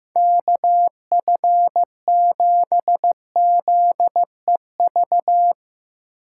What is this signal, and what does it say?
Morse code, heard as a keyed tone, about 16 wpm, 700 Hz. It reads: KF7ZEV
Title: Speed Racing - USA 2x3 Call Signs 15wpm